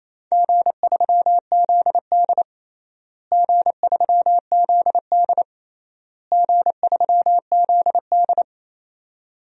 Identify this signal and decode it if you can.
Morse code, heard as a keyed tone, about 28 wpm, 700 Hz. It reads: G3ZD G3ZD G3ZD
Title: DXCC-100 Call Signs - 4 Characters - Repeated 3x - 28wpm